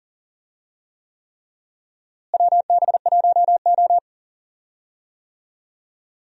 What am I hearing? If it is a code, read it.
Morse code, heard as a keyed tone, about 37 wpm, 700 Hz. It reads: WB1O
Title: USA Call Signs - 4 Characters - Repeated 3x - 40wpm